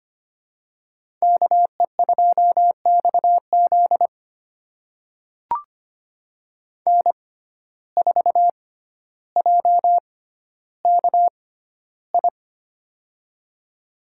Morse code, heard as a keyed tone, about 25 wpm, spaced as for 13 wpm, 690 Hz. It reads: KE2XZ N4JKI
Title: USA Call Signs - 5 Characters - 6x Character Spacing - 25wpm